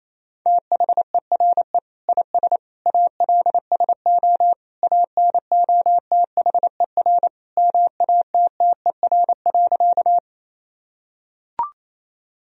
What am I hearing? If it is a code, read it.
Morse code, heard as a keyed tone, about 28 wpm, 700 Hz. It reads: THERE IS ALSO ANOTHER MATTER.
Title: Sentences from Top 1000 Words - Review Part 1 - 28wpm